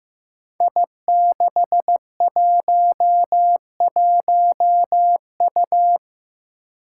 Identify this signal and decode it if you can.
Morse code, heard as a keyed tone, about 15 wpm, 700 Hz. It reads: I611U
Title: Five Challenging Letter-Number Combinations 15wpm